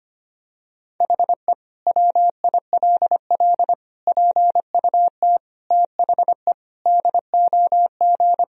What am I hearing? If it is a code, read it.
Morse code, heard as a keyed tone, about 25 wpm, 700 Hz. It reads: HE WILL PUT THE DOG
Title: Sentences from Top 1000 Words - Review Part 2 - 25wpm